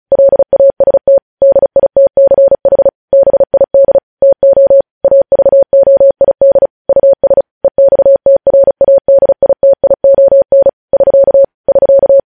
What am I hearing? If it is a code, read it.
Morse code, heard as a keyed tone, about 36 wpm, 550 Hz. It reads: LAST DITCH BID TO AVOID US EXTRADITION <SK> <SK>